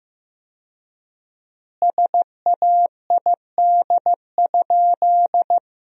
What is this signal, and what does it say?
Morse code, heard as a keyed tone, about 15 wpm, 700 Hz. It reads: SAID?
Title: 4x Word-Spacing - Sentences from Top 100 Words - 15wpm